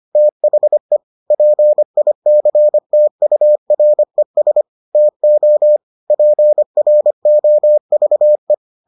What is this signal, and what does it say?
Morse code, heard as a keyed tone, about 25 wpm, 600 Hz. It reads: THE PICTURES TO PROVE